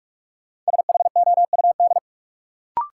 Morse code, heard as a keyed tone, about 46 wpm, 700 Hz. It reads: SHOUD
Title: Sentences from Top 1000 Words - Review Part 4 - New Word Drill - 45wpm